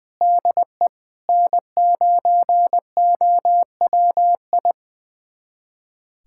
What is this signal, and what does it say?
Morse code, heard as a keyed tone, about 21 wpm, 700 Hz. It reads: DE N9OWI
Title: USA Callsign DE Callsign 20wpm